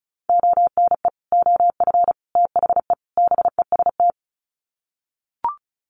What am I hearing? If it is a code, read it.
Morse code, heard as a keyed tone, about 34 wpm, 700 Hz. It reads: ONE OF THE BEST
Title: Sets of 4 Words 35wpm